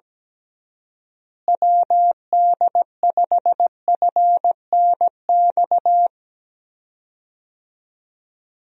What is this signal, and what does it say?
Morse code, heard as a keyed tone, about 17 wpm, 700 Hz. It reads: WD5FNX
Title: USA Call Signs - 6 Characters - Repeated 2x - 17wpm